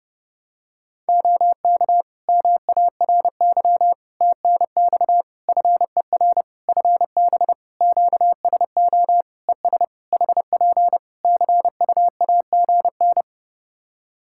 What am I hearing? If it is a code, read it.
Morse code, heard as a keyed tone, about 30 wpm, 700 Hz. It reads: OK MARY TNX FER FB QSO ES HP CUAGN
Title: Practice QSO - Exchange 4 - Ending - 30wpm